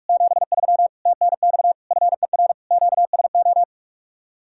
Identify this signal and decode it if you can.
Morse code, heard as a keyed tone, about 44 wpm, 700 Hz. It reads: 73 TNX FER QSO